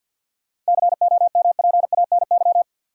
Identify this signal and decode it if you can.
Morse code, heard as a keyed tone, about 51 wpm, 700 Hz. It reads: COMPANY
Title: Sentences from Top 600 Words - New Word Drill - 50wpm